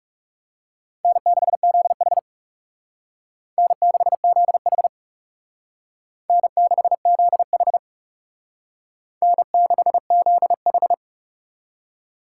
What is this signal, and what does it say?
Morse code, heard as a keyed tone, about 37 wpm, 700 Hz. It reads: N6ZH N6ZH N6ZH N6ZH
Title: Speed Racing - USA 1x2 Call Signs 30wpm